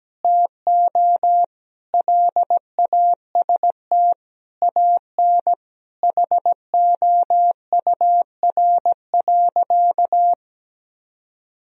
Morse code, heard as a keyed tone, about 17 wpm, 700 Hz. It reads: TO LAST AN HOUR.
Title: Sentences from Top 1000 Words - Review Part 1 - 17wpm